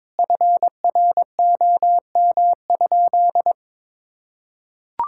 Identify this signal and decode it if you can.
Morse code, heard as a keyed tone, about 21 wpm, 700 Hz. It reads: FROM?
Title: Sentences from Top 1000 Words - Review Part 5 - 22wpm